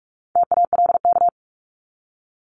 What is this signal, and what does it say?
Morse code, heard as a keyed tone, about 44 wpm, 700 Hz. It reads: TALK
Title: Top 500 Words 45wpm